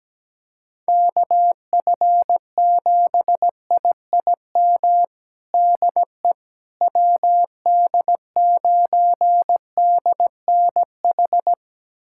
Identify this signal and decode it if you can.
Morse code, heard as a keyed tone, about 17 wpm, 700 Hz. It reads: KF7IIM DE WD9DNH
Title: USA Callsign DE Callsign - 6 Characters - 17wpm